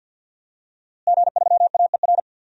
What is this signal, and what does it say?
Morse code, heard as a keyed tone, about 47 wpm, 700 Hz. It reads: G3AER
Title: DXCC-100 Call Signs - 5 Characters - Repeated 3x - 50wpm